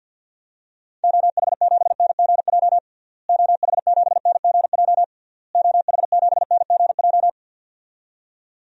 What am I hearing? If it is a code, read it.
Morse code, heard as a keyed tone, about 53 wpm, 700 Hz. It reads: OH7NGJ OH7NGJ OH7NGJ
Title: DXCC-100 Call Signs - 6 Characters - Repeated 3x - 50wpm